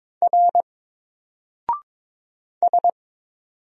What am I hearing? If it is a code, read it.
Morse code, heard as a keyed tone, about 22 wpm, 700 Hz. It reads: R S
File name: Single-Letter-Number-22wpm